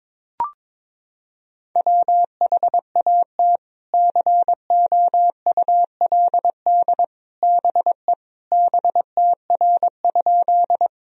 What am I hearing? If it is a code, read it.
Morse code, heard as a keyed tone, about 22 wpm, 700 Hz. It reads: WHAT COULD BE BTR?
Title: Sentences from Top 400 Words and Abbreviations 22wpm